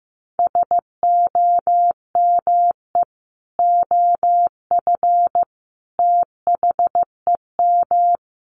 Morse code, heard as a keyed tone, about 15 wpm, 700 Hz. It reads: SOME OF THEM
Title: Sentences from Top 1200 Words 15wpm